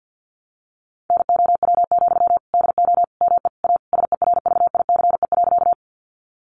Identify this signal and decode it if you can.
Morse code, heard as a keyed tone, about 50 wpm, 700 Hz. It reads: NOW, DO ME A SERVICE.
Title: Sentences from Top 700 Words 50wpm